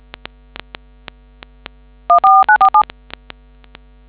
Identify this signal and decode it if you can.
DTMF, heard as telephone touch tones, tones of about 100 ms. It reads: 14947